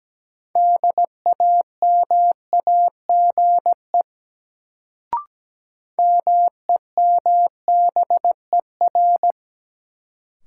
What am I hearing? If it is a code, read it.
Morse code, heard as a keyed tone, about 17 wpm, 700 Hz. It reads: DAMAGE MEMBER
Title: Sentences from Top 1100 Words - New Word Drill - 17wpm